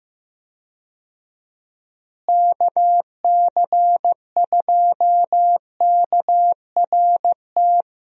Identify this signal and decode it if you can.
Morse code, heard as a keyed tone, about 15 wpm, 700 Hz. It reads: KC2KRT
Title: USA Call Signs - 6 Characters - Repeated 2x - 15wpm